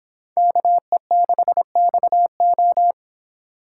Morse code, heard as a keyed tone, about 26 wpm, 700 Hz. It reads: KE6XO